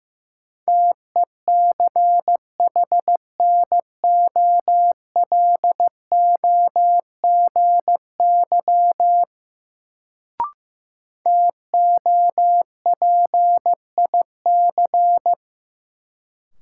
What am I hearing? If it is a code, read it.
Morse code, heard as a keyed tone, about 15 wpm, 700 Hz. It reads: TECHNOLOGY TOPIC
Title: Sentences from Top 1200 Words - New Word Drill Alt - 15wpm